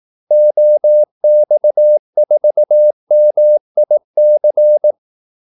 Morse code, heard as a keyed tone, about 18 wpm, 600 Hz. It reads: OX4MIC
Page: Kaldesignaler 18 wpm